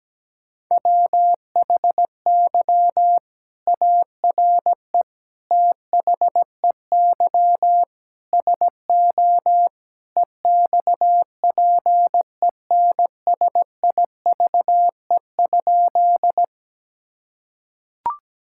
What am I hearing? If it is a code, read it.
Morse code, heard as a keyed tone, about 17 wpm, 700 Hz. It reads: WHY ARE THEY SO EXPENSIVE?
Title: Sentences from Top 1000 Words - Review Part 6 - 17wpm